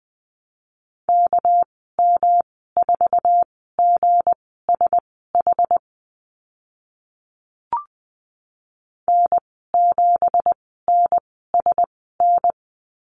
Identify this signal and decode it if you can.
Morse code, heard as a keyed tone, about 20 wpm, spaced as for 14 wpm, 700 Hz. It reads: KM4GSH N7NSN
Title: 219 Lesson 35g - Number 0 - Call Signs 20wpm